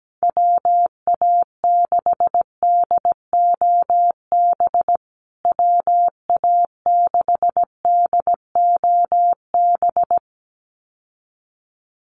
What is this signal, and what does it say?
Morse code, heard as a keyed tone, about 17 wpm, 700 Hz. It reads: WA6DOB WA6DOB
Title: USA Call Signs Repeated 2x 17wpm